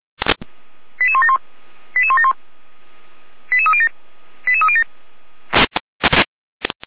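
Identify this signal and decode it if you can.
Five-tone selective call (ZVEI), heard as quick runs of five tones, tones of about 70 ms. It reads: 88171 88171 88307 88307
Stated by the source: ZVEI five-tone call received by SDR, recorded by Feuerwehr Orlamünde.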